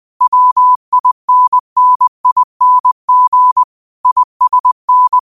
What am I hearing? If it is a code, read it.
Morse code, heard as a keyed tone, about 20 wpm, 1000 Hz. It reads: WINNING ISN